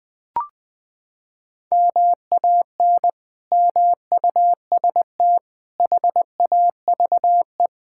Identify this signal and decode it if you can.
Morse code, heard as a keyed tone, about 20 wpm, 700 Hz. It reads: MAN MUST HAVE
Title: Sentences from Top 1000 Words - Review Part 6 - 20wpm